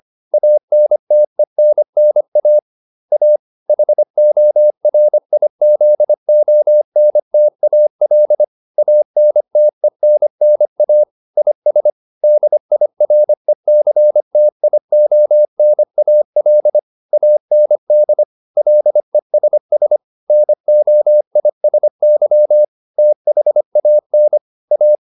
Morse code, heard as a keyed tone, about 25 wpm, 600 Hz. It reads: ANTENNA A HORIZONTAL ANTENNA IS DIRECTIONAL AND LESS NOISY THAN A
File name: SS202 25